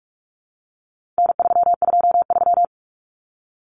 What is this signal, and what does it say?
Morse code, heard as a keyed tone, about 44 wpm, 700 Hz. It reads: N323
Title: Common Cut Numbers - 4-Digit 45wpm